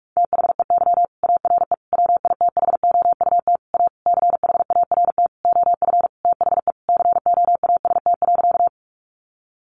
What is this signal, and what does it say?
Morse code, heard as a keyed tone, about 44 wpm, 700 Hz. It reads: THEY ARE WITHOUT A CHART OF THE COAST.
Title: Sentences from Top 1300 Words 45wpm